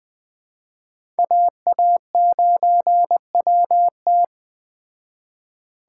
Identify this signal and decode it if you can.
Morse code, heard as a keyed tone, about 20 wpm, 700 Hz. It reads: AA9WT
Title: Speed Racing - USA 2x2 Call Signs 17wpm